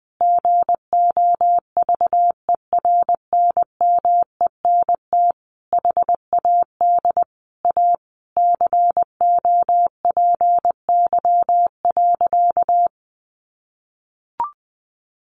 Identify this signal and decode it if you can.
Morse code, heard as a keyed tone, about 19 wpm, 700 Hz. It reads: GOVERNMENT HAD A COPY.
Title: Sentences from Top 1400 Words 20wpm